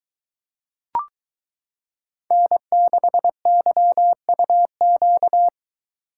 Morse code, heard as a keyed tone, about 23 wpm, 700 Hz. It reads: N6YUQ